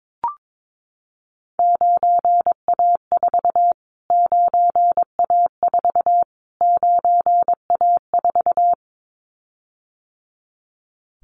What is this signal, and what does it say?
Morse code, heard as a keyed tone, about 22 wpm, 700 Hz. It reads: 9A4 9A4 9A4
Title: Callsign Prefixes for Top 25 DXCC - Repeated 3x - 22wpm